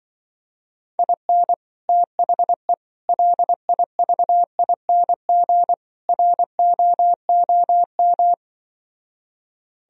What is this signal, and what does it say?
Morse code, heard as a keyed tone, about 24 wpm, 700 Hz. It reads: IN THE LIVING ROOM
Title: Speed-Racing - Sets of 4 Words 20wpm